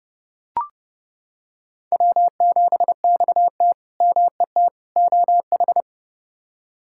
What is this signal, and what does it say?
Morse code, heard as a keyed tone, about 32 wpm, 700 Hz. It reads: W7XT MET OH
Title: K1USN SST QSO Practice - 30wpm